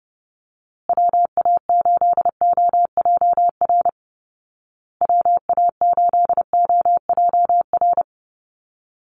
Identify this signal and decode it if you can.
Morse code, heard as a keyed tone, about 28 wpm, 700 Hz. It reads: WA8OJR WA8OJR
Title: Call Signs Repeated 2x - 30wpm